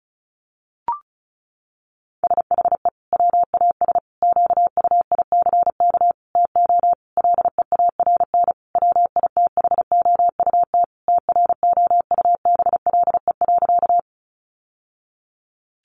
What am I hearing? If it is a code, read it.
Morse code, heard as a keyed tone, about 34 wpm, 700 Hz. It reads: SHE WAS QUICK TO LEARN WITHOUT TROUBLE.
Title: Sentences from Top 1200 Words 35wpm